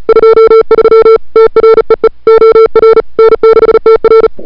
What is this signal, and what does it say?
Morse code, heard as a keyed tone, about 34 wpm, 440 Hz. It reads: 23 TREE OR N6TR
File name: n6tr / Fab2007CWSprintHour1